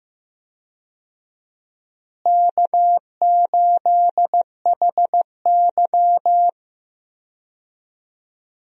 Morse code, heard as a keyed tone, about 15 wpm, 700 Hz. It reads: K8HY
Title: USA Call Signs - 4 Characters - Repeated 2x - 15wpm